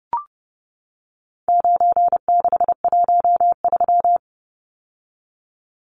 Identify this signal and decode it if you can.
Morse code, heard as a keyed tone, about 30 wpm, 700 Hz. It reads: 9613
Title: Numbers - 4-Digit 30wpm